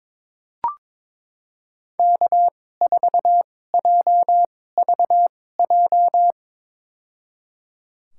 Morse code, heard as a keyed tone, about 22 wpm, spaced as for 16 wpm, 700 Hz. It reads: K4JVJ
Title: USA Call Signs - 5 Characters - 2x Character Spacing - 22wpm